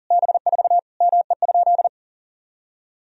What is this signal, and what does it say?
Morse code, heard as a keyed tone, about 40 wpm, 700 Hz. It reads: B4 ME?
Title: Sentences from Top 200 Words and Abbreviations 40wpm